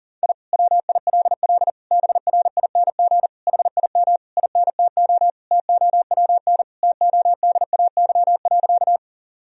Morse code, heard as a keyed tone, about 40 wpm, 700 Hz. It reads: I WILL BRING HIM INTO TOWN TODAY.
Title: Sentences from Top 1000 Words - Review Part 3 - 40wpm